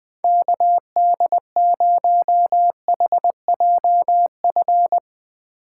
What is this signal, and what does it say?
Morse code, heard as a keyed tone, about 20 wpm, 700 Hz. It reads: KD0HJF
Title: Speed Racing - USA 2x3 Call Signs 20wpm